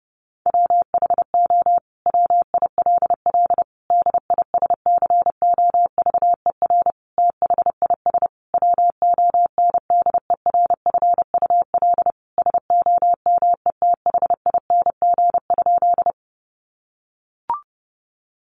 Morse code, heard as a keyed tone, about 30 wpm, 700 Hz. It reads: WHO WILL DISCOVER THIS WONDERFUL SOMETHING?
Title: Sentences from Top 1100 Words 30wpm